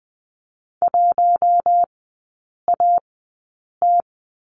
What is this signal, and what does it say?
Morse code, heard as a keyed tone, about 20 wpm, spaced as for 8 wpm, 700 Hz. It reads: 1AT
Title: 053 Lesson 10e - Letter r - Three Characters 20wpm